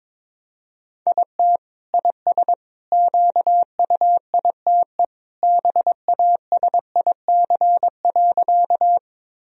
Morse code, heard as a keyed tone, about 21 wpm, 700 Hz. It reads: IT IS QUITE BASIC.
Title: Sentences from Top 1000 Words - Review Part 3 - 22wpm